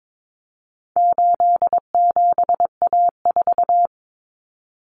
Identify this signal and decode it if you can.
Morse code, heard as a keyed tone, about 22 wpm, 700 Hz. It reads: 87A4
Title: Common Cut Numbers - 4-Digit 22wpm